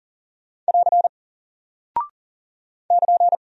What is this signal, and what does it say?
Morse code, heard as a keyed tone, about 39 wpm, 700 Hz. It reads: <AR> <KN>
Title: Single Character (Letter:Number:Punctuation:Prosign) Alt 40wpm